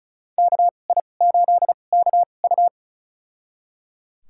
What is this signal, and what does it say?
Morse code, heard as a keyed tone, about 34 wpm, spaced as for 25 wpm, 700 Hz. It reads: KI8KU
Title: USA Call Signs - 5 Characters - 2x Character Spacing - 35wpm